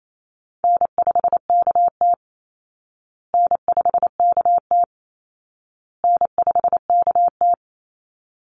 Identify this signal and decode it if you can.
Morse code, heard as a keyed tone, about 28 wpm, 700 Hz. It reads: N5KT N5KT N5KT
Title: Easy Call Signs Repeated 3x - 28wpm